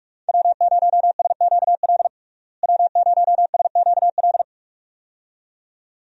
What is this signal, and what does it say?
Morse code, heard as a keyed tone, about 46 wpm, 700 Hz. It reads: W0SQL W0SQL
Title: USA Call Signs - 5 Characters - Repeated 2x - 45wpm